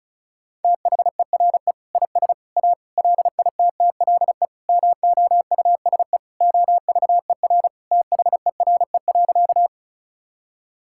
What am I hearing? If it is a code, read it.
Morse code, heard as a keyed tone, about 34 wpm, 700 Hz. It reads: THERE IS A LITTLE MOUSE OVER THERE.
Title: Sentences from Top 1000 Words - Review Part 5 - 35wpm